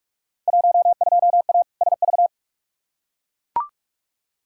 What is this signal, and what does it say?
Morse code, heard as a keyed tone, about 42 wpm, 700 Hz. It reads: 12A SV
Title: Field Day Exchanges Encore - 45wpm